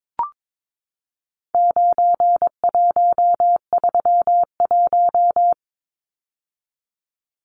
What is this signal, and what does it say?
Morse code, heard as a keyed tone, about 22 wpm, 700 Hz. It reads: 9131
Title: Numbers - 4-Digit 22wpm